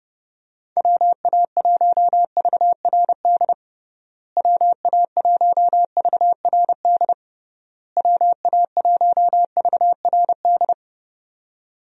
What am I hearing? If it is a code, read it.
Morse code, heard as a keyed tone, about 29 wpm, 700 Hz. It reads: WA1VRD WA1VRD WA1VRD
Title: USA Call Signs - 6 Characters - Repeated 3x - 30wpm